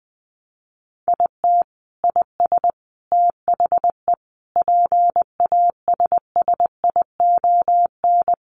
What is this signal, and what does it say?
Morse code, heard as a keyed tone, about 20 wpm, 700 Hz. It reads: IT IS THE PASSION
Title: Sentences from Top 1300 Words 20wpm